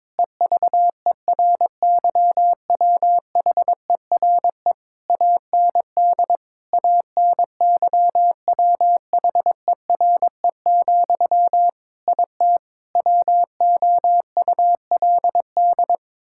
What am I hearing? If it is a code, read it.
Morse code, heard as a keyed tone, about 22 wpm, 700 Hz. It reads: EVERYWHERE AND ANYWHERE, IT WOULD
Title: Sentences from Top 1600 Words 22wpm